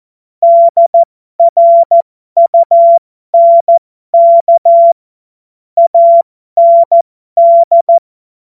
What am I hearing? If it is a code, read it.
Morse code, heard as a keyed tone, about 14 wpm, spaced as for 12 wpm, 680 Hz. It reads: DRUNK AND
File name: Morsekurs / Witze